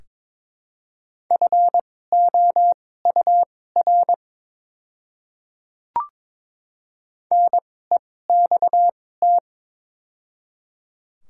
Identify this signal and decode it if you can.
Morse code, heard as a keyed tone, about 22 wpm, spaced as for 16 wpm, 700 Hz. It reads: FOUR NEXT
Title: 2x Character Spacing - Top 500 Words - 22wpm